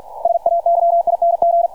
Morse code, heard as a keyed tone, about 50 wpm, 680 Hz, with radio noise. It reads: SAYING
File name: archived / CRSnet